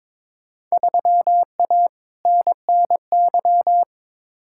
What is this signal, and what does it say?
Morse code, heard as a keyed tone, about 21 wpm, 700 Hz. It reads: 3A NNY
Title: Field Day Exchanges - 22wpm